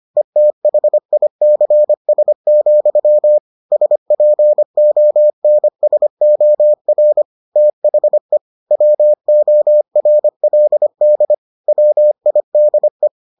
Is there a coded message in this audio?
Yes: Morse code from 0.2 to 13.1 s, heard as a keyed tone, about 25 wpm, 600 Hz. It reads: ETHICS, SPONSOR THE WORLD WIDE